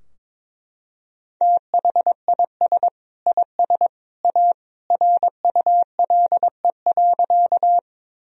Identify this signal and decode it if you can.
Morse code, heard as a keyed tone, about 22 wpm, 700 Hz. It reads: THIS IS A RULE.
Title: Sentences from Top 1000 Words - Review Part 2 - 22wpm